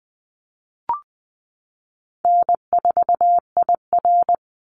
Morse code, heard as a keyed tone, about 19 wpm, 700 Hz. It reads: N4IR
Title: Easy Call Signs - 20wpm